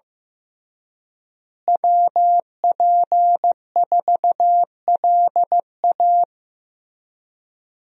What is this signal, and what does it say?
Morse code, heard as a keyed tone, about 15 wpm, 700 Hz. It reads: WP4LA
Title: USA Call Signs - 5 Characters - Repeated 3x - 15wpm